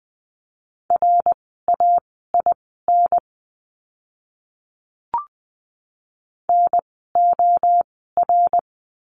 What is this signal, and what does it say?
Morse code, heard as a keyed tone, about 20 wpm, spaced as for 14 wpm, 700 Hz. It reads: RAIN NOR
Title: 068 Lesson 12f - Letter d - Words 20wpm